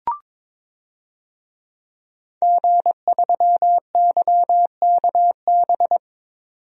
Morse code, heard as a keyed tone, about 22 wpm, 700 Hz. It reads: G3YKB
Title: DXCC-100 Call Signs - 5 Characters - 22wpm